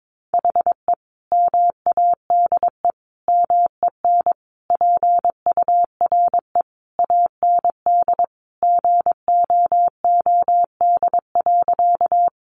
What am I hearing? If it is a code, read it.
Morse code, heard as a keyed tone, about 22 wpm, 700 Hz. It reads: HE MADE MEN PURE AND GOOD.
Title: Sentences from Top 1500 Words 22wpm